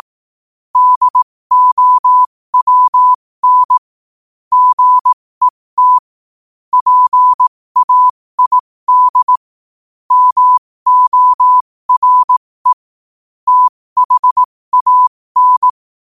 Morse code, heard as a keyed tone, about 18 wpm, spaced as for 15 wpm, 1000 Hz. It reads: DOWN GET PAID MORE THAN
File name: qotdfiles / 15wpm